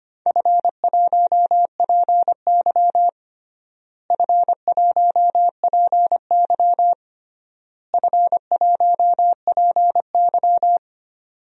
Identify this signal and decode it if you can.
Morse code, heard as a keyed tone, about 25 wpm, 700 Hz. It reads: F1PY F1PY F1PY
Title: DXCC-100 Call Signs - 4 Characters - Repeated 3x - 25wpm